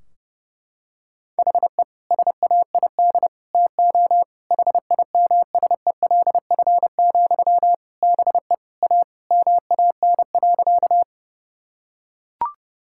Morse code, heard as a keyed tone, about 30 wpm, 700 Hz. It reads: HE SAID TO HIMSELF, BE A MAN.
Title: Sentences from Top 1600 Words 30wpm